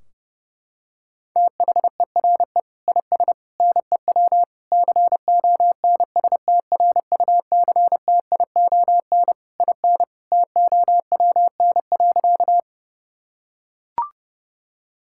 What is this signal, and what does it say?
Morse code, heard as a keyed tone, about 30 wpm, 700 Hz. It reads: THERE IS NEW CONSTRUCTION IN TOWN.
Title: Sentences from Top 1600 Words 30wpm